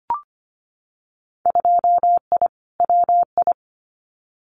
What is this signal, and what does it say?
Morse code, heard as a keyed tone, about 25 wpm, 700 Hz. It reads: 2I WI
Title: Winter Field Day Exchanges 25wpm